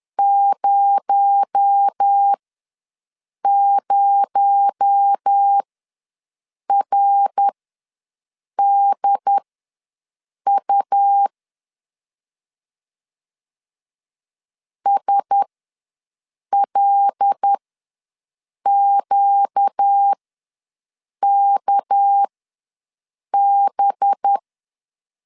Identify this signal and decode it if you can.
Morse code, heard as a keyed tone, about 10 wpm, spaced as for 6 wpm, 800 Hz. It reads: 00RDU SLQKB
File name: LEZIONI / 6 PKX0 / 6 PKX0 ws 10wpm 4220